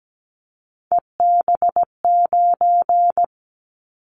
Morse code, heard as a keyed tone, about 17 wpm, 700 Hz. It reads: EB9
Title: Callsign Prefixes for Top 100 DXCC - 17wpm